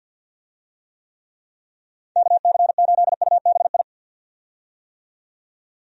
Morse code, heard as a keyed tone, about 51 wpm, 700 Hz. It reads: KC8UBI
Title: USA Call Signs - 6 Characters - 50wpm